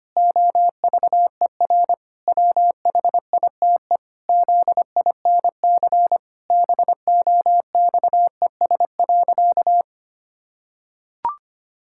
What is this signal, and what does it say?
Morse code, heard as a keyed tone, about 25 wpm, 700 Hz. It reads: OVER WHITE ZINC BOXES.
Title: Pangram Challenge v2 - 25wpm